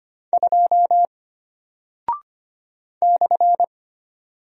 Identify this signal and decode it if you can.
Morse code, heard as a keyed tone, about 25 wpm, 700 Hz. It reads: 2 /
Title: Single Character (Letter:Number:Punctuation:Prosign) Alt 25wpm